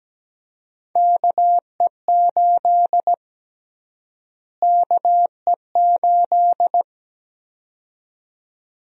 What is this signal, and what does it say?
Morse code, heard as a keyed tone, about 17 wpm, 700 Hz. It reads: KE8 KE8
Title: USA Call Sign Prefixes - Repeated 2x - 17wpm